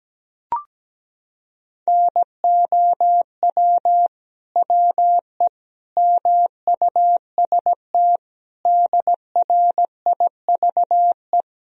Morse code, heard as a keyed tone, about 17 wpm, 700 Hz. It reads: NOW WE MUST DRIVE
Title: Sentences from Top 1000 Words - Review Part 2 - 17wpm